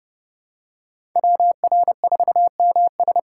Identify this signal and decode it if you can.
Morse code, heard as a keyed tone, about 30 wpm, 700 Hz. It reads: WR4MS
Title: USA Call Signs - 5 Characters - 6x Character Spacing - 30wpm